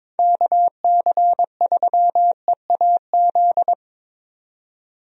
Morse code, heard as a keyed tone, about 22 wpm, 700 Hz. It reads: KC3EAZ
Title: USA Call Signs - 6 Characters - 22wpm